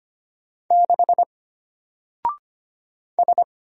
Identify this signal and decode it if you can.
Morse code, heard as a keyed tone, about 25 wpm, 700 Hz. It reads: 6 S
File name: Single-Letter-Number-25wpm